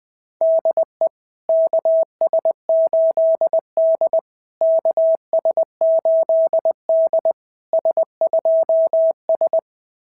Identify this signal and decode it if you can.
Morse code, heard as a keyed tone, about 20 wpm, 660 Hz. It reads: DE KS8D KS8D S2S
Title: SOTA QSO Practice 20wpm